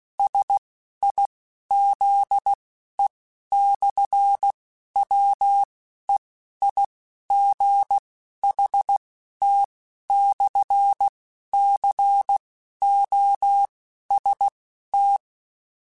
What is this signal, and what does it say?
Morse code, heard as a keyed tone, about 16 wpm, spaced as for 11 wpm, 800 Hz. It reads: SIZE/WEIGHT/COST
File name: Technical Subjects / Technical Subjects-mp3s / Sw Pwr Sup @10